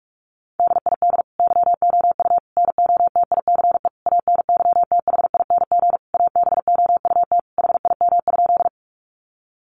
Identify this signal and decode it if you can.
Morse code, heard as a keyed tone, about 44 wpm, 700 Hz. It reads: DID YOU NOTICE ANYTHING ABOUT HIM?
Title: Sentences from Top 1000 Words 45wpm